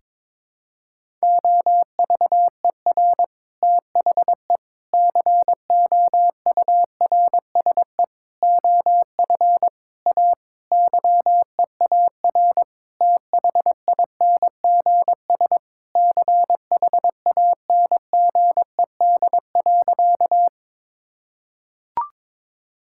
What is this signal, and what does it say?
Morse code, heard as a keyed tone, about 22 wpm, 700 Hz. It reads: OVER THE COURSE OF A YEAR THINGS CHANGED.
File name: practice / 22wpm / Sentences with 4-Word Phrases - 22wpm